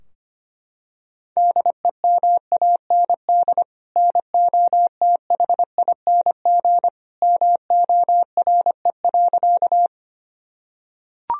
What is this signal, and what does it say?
Morse code, heard as a keyed tone, about 25 wpm, 700 Hz. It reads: DEMAND NOTHING MORE.
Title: Sentences from Top 1000 Words - Review Part 4 - 25wpm